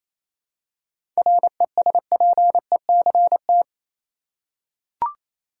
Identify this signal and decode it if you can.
Morse code, heard as a keyed tone, about 28 wpm, 700 Hz. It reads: RESPECT
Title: Seven Letter Words - 28wpm